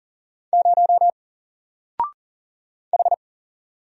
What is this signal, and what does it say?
Morse code, heard as a keyed tone, about 40 wpm, 700 Hz. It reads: 0 H
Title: Single Letter-Number Alt 40wpm